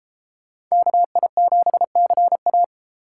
Morse code, heard as a keyed tone, about 35 wpm, 700 Hz. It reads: KI7CA
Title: Speed Racing - USA Call Signs 28wpm